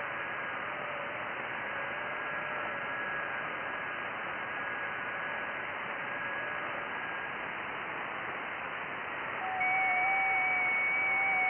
FT8.
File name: IC-7100